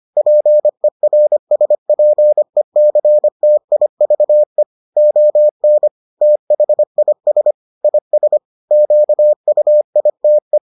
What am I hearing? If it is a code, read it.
Morse code, heard as a keyed tone, about 25 wpm, 600 Hz. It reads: PERSPECTIVE ON THIS IS QUITE